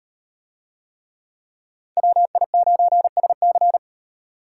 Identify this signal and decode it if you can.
Morse code, heard as a keyed tone, about 38 wpm, 700 Hz. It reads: WI9SC